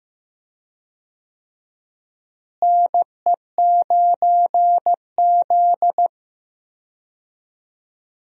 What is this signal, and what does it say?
Morse code, heard as a keyed tone, about 15 wpm, 700 Hz. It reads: NE9Z
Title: USA Call Signs - 4 Characters - Repeated 2x - 15wpm